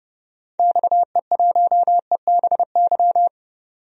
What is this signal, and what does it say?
Morse code, heard as a keyed tone, about 31 wpm, 700 Hz. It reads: XE1EBY
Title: DXCC-100 Call Signs - 6 Characters - 30wpm